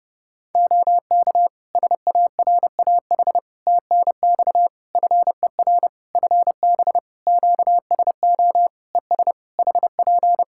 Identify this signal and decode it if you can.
Morse code, heard as a keyed tone, about 30 wpm, 700 Hz. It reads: OK SARAH TNX FER FB QSO ES HP
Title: Practice QSO - Exchange 4 - Ending - 30wpm